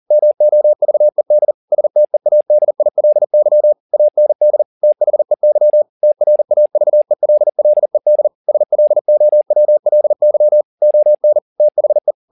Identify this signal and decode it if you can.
Morse code, heard as a keyed tone, about 40 wpm, 600 Hz. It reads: MOVED STEADILY AND THEY TRAVELLED SLOWLY ON THE